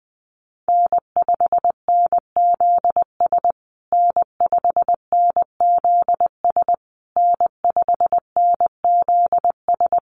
Morse code, heard as a keyed tone, about 20 wpm, 700 Hz. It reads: N5NZS N5NZS N5NZS
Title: USA Call Signs Repeated 3x 20wpm